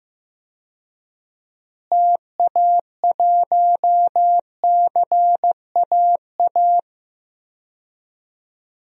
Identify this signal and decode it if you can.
Morse code, heard as a keyed tone, about 15 wpm, 700 Hz. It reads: TA1CAA
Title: DXCC-100 Call Signs - 6 Characters - Repeated 2x - 15wpm